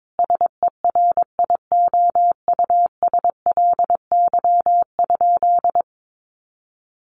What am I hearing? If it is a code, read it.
Morse code, heard as a keyed tone, about 22 wpm, 700 Hz. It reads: SERIOUSLY?
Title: Sentences from Top 1200 Words 22wpm